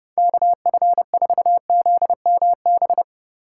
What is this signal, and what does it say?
Morse code, heard as a keyed tone, about 36 wpm, 700 Hz. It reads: KF4ZMB
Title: Speed Racing - USA 2x3 Call Signs 30wpm